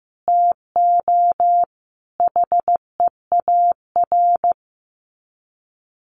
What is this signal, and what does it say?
Morse code, heard as a keyed tone, about 15 wpm, 700 Hz. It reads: TO HEAR
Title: Sets of 5 Words 15wpm